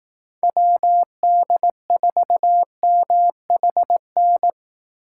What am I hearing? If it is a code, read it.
Morse code, heard as a keyed tone, about 18 wpm, 700 Hz. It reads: WD4MHN
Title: Speed Racing - USA 2x3 Call Signs 15wpm